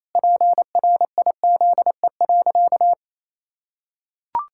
Morse code, heard as a keyed tone, about 28 wpm, 700 Hz. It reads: PRIZE.
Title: Pangram Challenge v2 - 28wpm